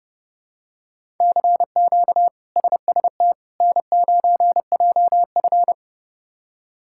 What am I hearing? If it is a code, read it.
Morse code, heard as a keyed tone, about 30 wpm, 700 Hz. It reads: CQ SST N9JF
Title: K1USN SST QSO Practice - 30wpm